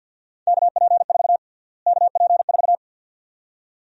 Morse code, heard as a keyed tone, about 53 wpm, 700 Hz. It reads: KP4 KP4
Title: USA Call Sign Prefixes - Repeated 2x - 50wpm